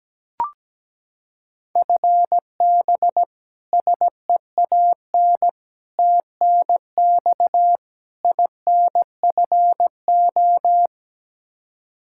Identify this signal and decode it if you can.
Morse code, heard as a keyed tone, about 17 wpm, 700 Hz. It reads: FB SEAN TNX INFO
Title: Practice QSO - Exchange 3 - More Info - 17wpm